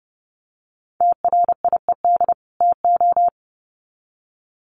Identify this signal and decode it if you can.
Morse code, heard as a keyed tone, about 30 wpm, 700 Hz. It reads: TRIED TO
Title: Sets of 2 Words 30wpm